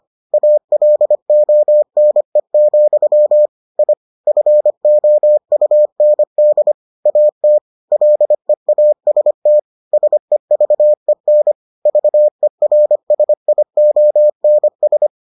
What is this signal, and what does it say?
Morse code, heard as a keyed tone, about 25 wpm, 600 Hz. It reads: ALONE, I FOUND AT LEAST SEVEN VERSIONS